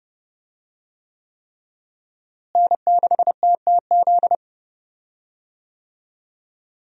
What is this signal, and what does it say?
Morse code, heard as a keyed tone, about 30 wpm, 700 Hz. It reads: N6TTZ
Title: USA Call Signs - 5 Characters - Repeated 3x - 30wpm